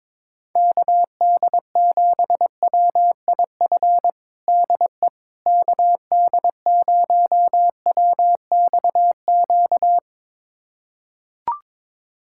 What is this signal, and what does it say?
Morse code, heard as a keyed tone, about 22 wpm, 700 Hz. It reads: KD7WIF DE KD0WXQ
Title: USA Callsign DE Callsign - 6 Characters - 22wpm